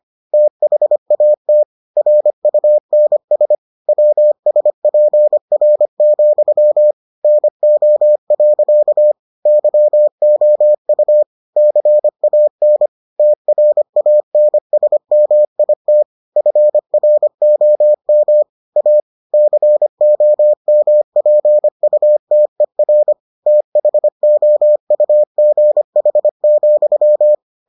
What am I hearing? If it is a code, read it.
Morse code, heard as a keyed tone, about 25 wpm, 600 Hz. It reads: THAT RUNS WSPR, NO. YOU CAN TRANSMIT FROM A COMPUTER THOUGH,